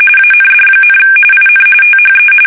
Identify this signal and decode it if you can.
RTTY, about 45.45 baud. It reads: AND, IN THE
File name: fsk850 / usnavy2